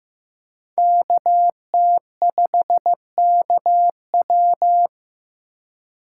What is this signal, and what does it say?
Morse code, heard as a keyed tone, about 18 wpm, 700 Hz. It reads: KT5KW
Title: Speed Racing - USA Call Signs 15wpm